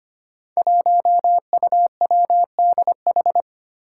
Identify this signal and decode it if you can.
Morse code, heard as a keyed tone, about 25 wpm, 700 Hz. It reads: 1UWDH
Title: Five Challenging Letter-Number Combinations 25wpm